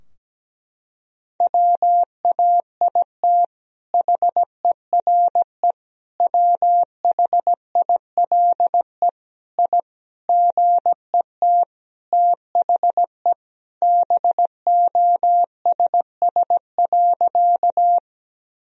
Morse code, heard as a keyed tone, about 17 wpm, 700 Hz. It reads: WAIT HERE WHILE I GET THE BOSS.
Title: Sentences from Top 1000 Words - Review Part 4 - 17wpm